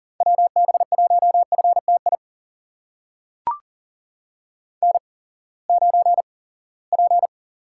Morse code, heard as a keyed tone, about 40 wpm, spaced as for 23 wpm, 700 Hz. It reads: WB1FTI N9P
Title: USA Call Signs - 8x Character Spacing - 40wpm